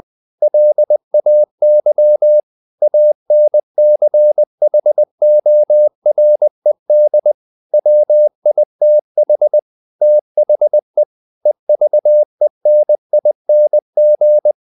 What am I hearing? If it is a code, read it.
Morse code, heard as a keyed tone, about 20 wpm, 600 Hz. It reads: LAY ANCHORED WITH THE EVENING
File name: the old man and the sea part 2 20wpm